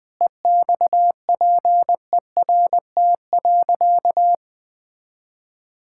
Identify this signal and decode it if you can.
Morse code, heard as a keyed tone, about 21 wpm, 700 Hz. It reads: EXPERT.
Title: Sentences with 5-Word Phrases - Spoken Beginning - 20wpm